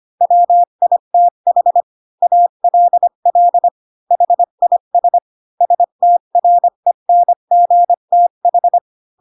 Morse code, heard as a keyed tone, about 25 wpm, 700 Hz. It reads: WITH ALL HIS STRENGTH
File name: Pinocchio / Pinocchio ch4 20 25 ik1hge